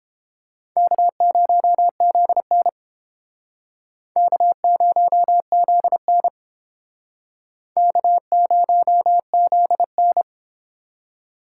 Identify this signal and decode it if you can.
Morse code, heard as a keyed tone, about 31 wpm, 700 Hz. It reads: K0ZN K0ZN K0ZN